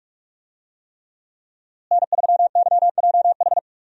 Morse code, heard as a keyed tone, about 46 wpm, 700 Hz. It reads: N3YJH
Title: USA Call Signs - 5 Characters - Repeated 3x - 45wpm